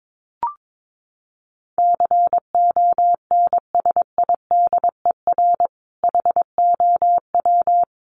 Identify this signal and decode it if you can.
Morse code, heard as a keyed tone, about 22 wpm, 700 Hz. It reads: CONSIDER HOW
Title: Sentences from Top 1000 Words 22wpm